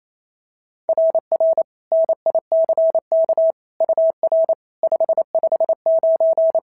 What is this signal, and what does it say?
Morse code, heard as a keyed tone, about 28 wpm, 690 Hz. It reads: RR NICK UR 559
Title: SOTA QSO Practice 28wpm